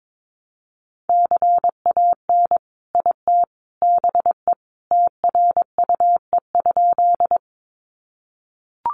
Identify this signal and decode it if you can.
Morse code, heard as a keyed tone, about 22 wpm, 700 Hz. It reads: CAN IT BE TRUE?
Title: Sentences from Top 400 Words 22wpm